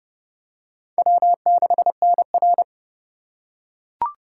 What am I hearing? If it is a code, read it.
Morse code, heard as a keyed tone, about 29 wpm, 700 Hz. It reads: W6NR